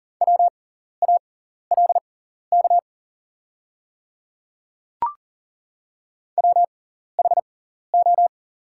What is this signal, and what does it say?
Morse code, heard as a keyed tone, about 41 wpm, spaced as for 14 wpm, 700 Hz. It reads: WALK WHO
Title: 6x Character Spacing - Top 500 Words - 40wpm